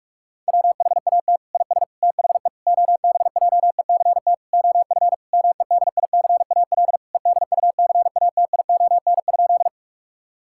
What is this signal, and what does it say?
Morse code, heard as a keyed tone, about 46 wpm, 700 Hz. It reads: WHAT IS THE OBJECT OF MEDICAL EDUCATION?
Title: Sentences from Top 1000 Words - Review Part 6 - 45wpm